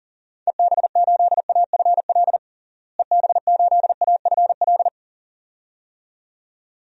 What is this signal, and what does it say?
Morse code, heard as a keyed tone, about 40 wpm, 700 Hz. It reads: EB8AFL EB8AFL
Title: DXCC-100 Call Signs - 6 Characters - Repeated 2x - 40wpm